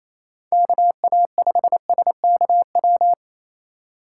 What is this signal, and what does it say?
Morse code, heard as a keyed tone, about 28 wpm, 700 Hz. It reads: KA5SKW
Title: USA Call Signs - 6 Characters - 8x Character Spacing - 28wpm